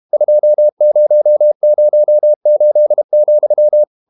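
Morse code, heard as a keyed tone, about 32 wpm, 600 Hz. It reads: 2008,
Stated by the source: Morse code practice at 32 wpm from CW med Gnister, reading Ken Follett's novel Never in English.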